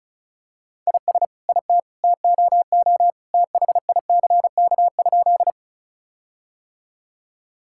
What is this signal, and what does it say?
Morse code, heard as a keyed tone, about 36 wpm, 700 Hz. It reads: IS IT TOO THICK?
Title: Sentences from Top 1600 Words 35wpm